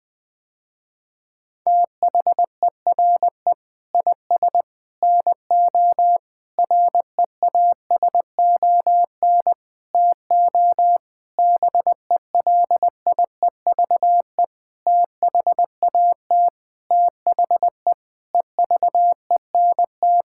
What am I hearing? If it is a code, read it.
Morse code, heard as a keyed tone, about 20 wpm, 700 Hz. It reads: THERE IS NO REASON TO BELIEVE THAT THE EVENT
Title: Sentences with 8-Word Phrases - Spoken Beginning - 20wpm